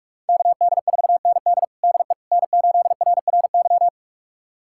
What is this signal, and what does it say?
Morse code, heard as a keyed tone, about 46 wpm, 700 Hz. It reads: KD4ND DE N8RRY
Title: USA Callsign DE Callsign - 5 Characters - 45wpm